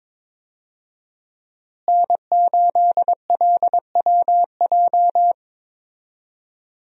Morse code, heard as a keyed tone, about 22 wpm, 700 Hz. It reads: N8LWJ
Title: USA Call Signs - 5 Characters - Repeated 3x - 22wpm